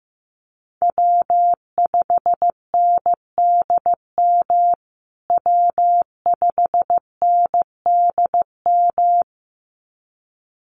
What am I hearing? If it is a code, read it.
Morse code, heard as a keyed tone, about 15 wpm, 700 Hz. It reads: W5NDM W5NDM
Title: USA Call Signs Repeated 2x 15wpm